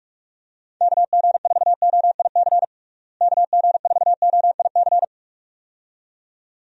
Morse code, heard as a keyed tone, about 41 wpm, 700 Hz. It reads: KG4OIC KG4OIC
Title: USA Call Signs - 6 Characters - Repeated 2x - 45wpm